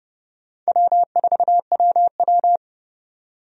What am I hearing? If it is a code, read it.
Morse code, heard as a keyed tone, about 29 wpm, 700 Hz. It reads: W4WW
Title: USA Call Signs - 4 Characters - 30wpm